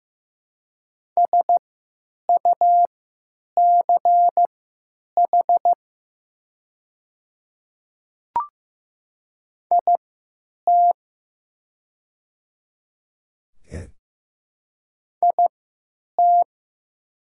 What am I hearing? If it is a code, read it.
Morse code, heard as a keyed tone, about 15 wpm, spaced as for 9 wpm, 700 Hz. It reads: SUCH IT IT
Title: 3x Character Spacing - Top 200 Words - 15wpm